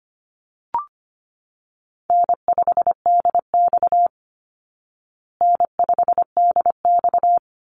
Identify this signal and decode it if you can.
Morse code, heard as a keyed tone, about 25 wpm, 700 Hz. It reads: N5DX N5DX
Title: Easy Call Signs Repeated 2x - 25wpm